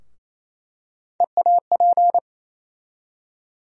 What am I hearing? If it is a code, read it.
Morse code, heard as a keyed tone, about 27 wpm, 700 Hz. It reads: EAP
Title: USA Call Sign Postfixes - 3 Letters 28wpm